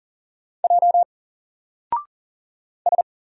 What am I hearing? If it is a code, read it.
Morse code, heard as a keyed tone, about 40 wpm, 700 Hz. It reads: J S
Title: Single Letter-Number 40wpm